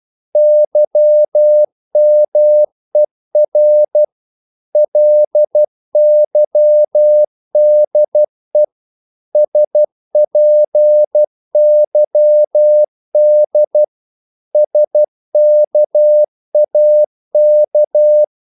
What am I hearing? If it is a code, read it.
Morse code, heard as a keyed tone, about 12 wpm, 600 Hz. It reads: YMER LYDE SPYD SKAK
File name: Korte ord DK 12wpm